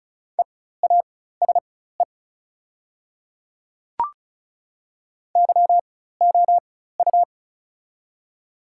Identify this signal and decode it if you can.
Morse code, heard as a keyed tone, about 34 wpm, spaced as for 16 wpm, 690 Hz. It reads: EASE YOU
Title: 4x Character Spacing - Top 300 Words - 35wpm